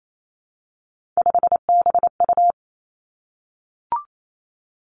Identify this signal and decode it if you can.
Morse code, heard as a keyed tone, about 28 wpm, 700 Hz. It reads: 5BU
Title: Three Challenging Letter-Number Combinations 28wpm